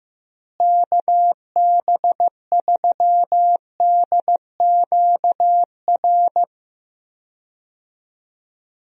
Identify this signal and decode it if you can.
Morse code, heard as a keyed tone, about 15 wpm, 700 Hz. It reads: KB3DQR
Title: USA Call Signs - 6 Characters - Repeated 3x - 15wpm